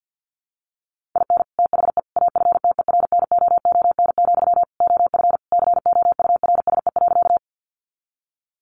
Morse code, heard as a keyed tone, about 50 wpm, 700 Hz. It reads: IN THE AFTERNOON, OF COURSE.
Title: Sentences from Top 1500 Words 50wpm